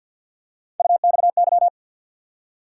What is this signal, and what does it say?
Morse code, heard as a keyed tone, about 52 wpm, 700 Hz. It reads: UXY
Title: USA Call Sign Postfixes - 3 Letters 50wpm